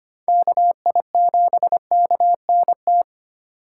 Morse code, heard as a keyed tone, about 25 wpm, 700 Hz. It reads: KI7KNT
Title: USA Call Signs - 6 Characters - 3x Character Spacing - 25wpm